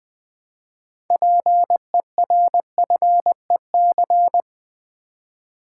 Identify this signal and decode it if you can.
Morse code, heard as a keyed tone, about 20 wpm, 700 Hz. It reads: PERFEC
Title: Build A Word - 3 Letters At A Time - 9-Letter Words - 20wpm